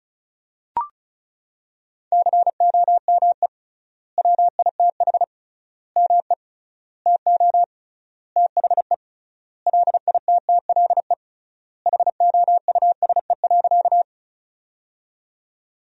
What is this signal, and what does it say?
Morse code, heard as a keyed tone, about 34 wpm, 700 Hz. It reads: COME WITH ME TO THE LITTLE HOUSE.
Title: 3x Word-Spacing - Sentences from Top 200 Words - 35wpm